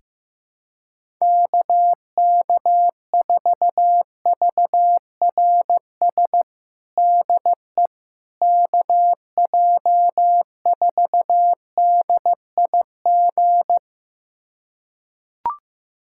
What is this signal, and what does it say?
Morse code, heard as a keyed tone, about 15 wpm, 700 Hz. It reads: KK4VRS DE KJ4DIG
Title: USA Callsign DE Callsign - 6 Characters - 15wpm